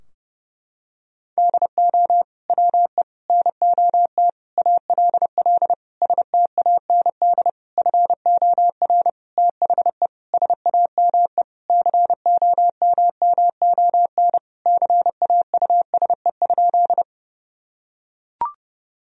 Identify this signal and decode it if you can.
Morse code, heard as a keyed tone, about 32 wpm, 700 Hz. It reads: DO WE NOT ALL STAND FOR THE SAME COMMON CAUSE?
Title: Sentences from Top 1000 Words - Review Part 3 - 30wpm